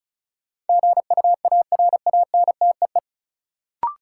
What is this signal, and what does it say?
Morse code, heard as a keyed tone, about 34 wpm, 700 Hz. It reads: GUARANTEE
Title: Sentences from Top 1600 Words - New Word Drill - 35wpm